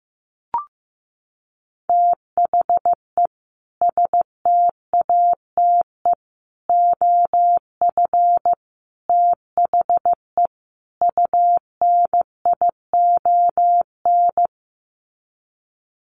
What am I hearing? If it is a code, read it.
Morse code, heard as a keyed tone, about 15 wpm, 700 Hz. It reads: THE STATE OF THE UNION
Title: Sets of 5 Words Encore 15wpm